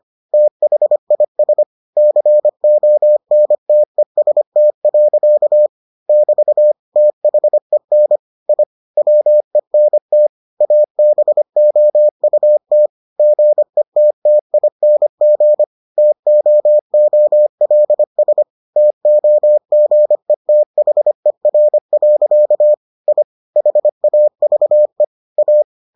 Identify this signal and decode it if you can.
Morse code, heard as a keyed tone, about 25 wpm, 600 Hz. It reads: THIS CONTEST. <BT> THEN I WENT ABOUT GETTING TOOLS TOGETHER. I HAVE A